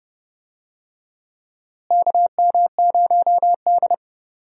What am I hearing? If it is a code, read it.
Morse code, heard as a keyed tone, about 30 wpm, 700 Hz. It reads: KM0D
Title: USA Call Signs - 4 Characters - Repeated 3x - 30wpm